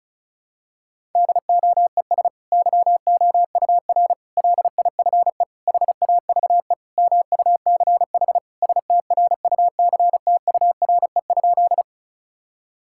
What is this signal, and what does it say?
Morse code, heard as a keyed tone, about 36 wpm, 700 Hz. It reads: DOES YOUR LIFE HAVE MUCH STRUCTURE?
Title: Sentences from Top 1000 Words - Review Part 6 - 35wpm